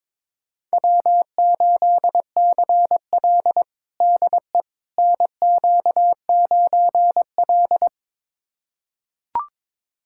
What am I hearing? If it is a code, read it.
Morse code, heard as a keyed tone, about 22 wpm, 700 Hz. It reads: W8CL DE NQ9L
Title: USA Callsign DE Callsign - 4 Characters - 22wpm